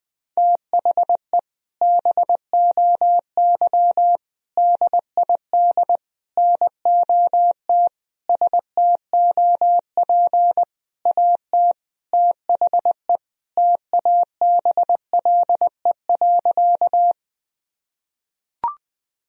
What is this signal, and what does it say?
Morse code, heard as a keyed tone, about 20 wpm, 700 Hz. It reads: THE BOY DID NOT STOP AT THE TABLE.
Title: Sentences from Top 1000 Words - Review Part 2 - 20wpm